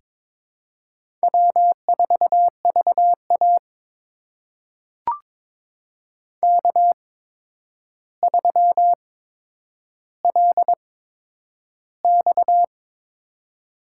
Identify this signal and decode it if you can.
Morse code, heard as a keyed tone, about 23 wpm, spaced as for 10 wpm, 700 Hz. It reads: W4VA K3LX
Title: USA Call Signs - 4 Characters - 8x Character Spacing - 22wpm